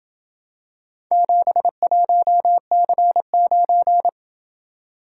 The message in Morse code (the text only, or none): 71C9